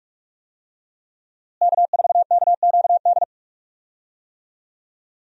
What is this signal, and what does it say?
Morse code, heard as a keyed tone, about 46 wpm, 700 Hz. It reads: K4KQD
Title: USA Call Signs - 5 Characters - Repeated 3x - 45wpm